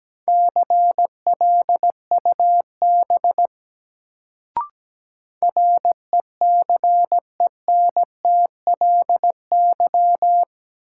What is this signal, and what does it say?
Morse code, heard as a keyed tone, about 17 wpm, 700 Hz. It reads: CLUB RECENTLY
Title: Sentences from Top 1200 Words - New Word Drill - 17wpm